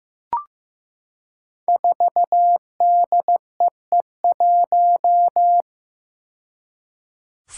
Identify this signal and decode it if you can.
Morse code, heard as a keyed tone, about 15 wpm, 700 Hz. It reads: 4DEE1
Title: Five Challenging Letter-Number Combinations 15wpm